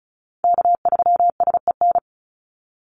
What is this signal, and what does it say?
Morse code, heard as a keyed tone, about 34 wpm, 700 Hz. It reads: K3SEN
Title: Call Signs Repeated 3x - 35wpm